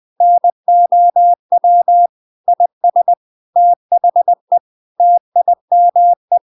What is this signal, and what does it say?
Morse code, heard as a keyed tone, about 20 wpm, 700 Hz. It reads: NOW IS THE TIME
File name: Now is the time - 20wpm